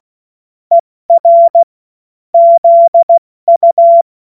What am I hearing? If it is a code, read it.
Morse code, heard as a keyed tone, about 16 wpm, 680 Hz. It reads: ER ZU
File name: Morsekurs / Witze / Witz 21.November 2023 70 F80